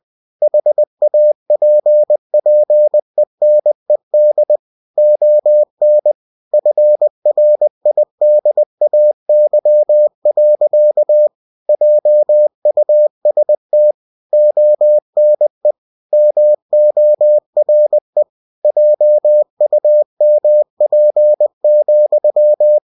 Morse code, heard as a keyed tone, about 20 wpm, 600 Hz. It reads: HAPPENED ON FRIDAY. JUST ONE MORE JUMP,
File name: cw-books / winnie-the-pooh / winnie-20wpm-page-0068